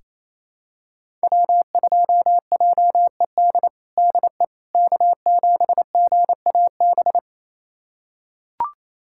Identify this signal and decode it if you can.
Morse code, heard as a keyed tone, about 28 wpm, 700 Hz. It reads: W2JED DE K7GAB
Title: USA Callsign DE Callsign - 5 Characters - 28wpm